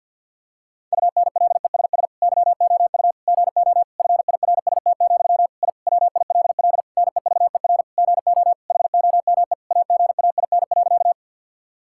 Morse code, heard as a keyed tone, about 50 wpm, 700 Hz. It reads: UNLESS YOU GO FIRST, I WILL NEVER GO HOME AGAIN.
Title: Sentences from Top 1000 Words - Review Part 6 - 50wpm